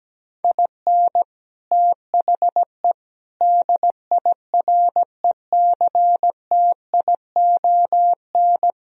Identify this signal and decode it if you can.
Morse code, heard as a keyed tone, about 17 wpm, 700 Hz. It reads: IN THE DIRECTION
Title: Speed-Racing - Sets of 4 Words 17wpm